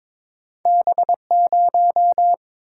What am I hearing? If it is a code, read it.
Morse code, heard as a keyed tone, about 22 wpm, 700 Hz. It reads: B0
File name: practice / 22wpm / Two Letter-Number Combinations - 22wpm